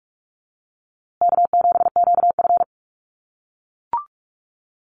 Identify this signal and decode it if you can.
Morse code, heard as a keyed tone, about 44 wpm, 700 Hz. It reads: K7QF
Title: USA Call Signs Repeated 2x 45wpm